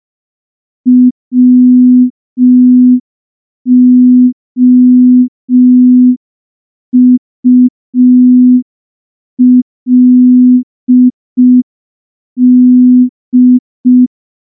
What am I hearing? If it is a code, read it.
Morse code, heard as a keyed tone, about 5 wpm, 250 Hz. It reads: WOULD
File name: morse / morse-training-05WPM